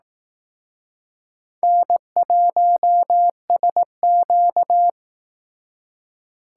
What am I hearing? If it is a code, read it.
Morse code, heard as a keyed tone, about 18 wpm, 700 Hz. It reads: N1SQ
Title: Speed Racing - USA 1x2 Call Signs 15wpm